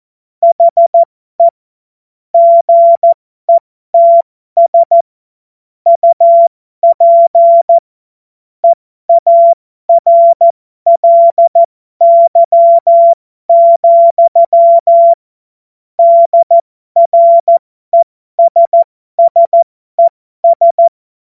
Morse code, heard as a keyed tone, about 14 wpm, spaced as for 12 wpm, 680 Hz. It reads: HE GETS UP EARLY, DRESSES
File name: Morsekurs / Witze